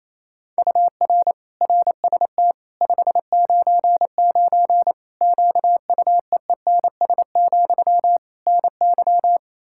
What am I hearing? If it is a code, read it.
Morse code, heard as a keyed tone, about 28 wpm, 700 Hz. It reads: UR RST 599 QUEENS, NY
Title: Practice QSO - Exchange 1 - Beginning - 28wpm